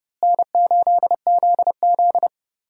Morse code, heard as a keyed tone, about 30 wpm, 700 Hz. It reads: N8ZZ
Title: USA Call Signs - 4 Characters - Repeated 2x - 30wpm